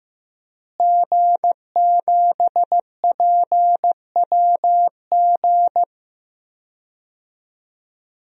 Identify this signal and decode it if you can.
Morse code, heard as a keyed tone, about 15 wpm, 700 Hz. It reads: G7PWG
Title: DXCC-100 Call Signs - 5 Characters - Repeated 3x - 15wpm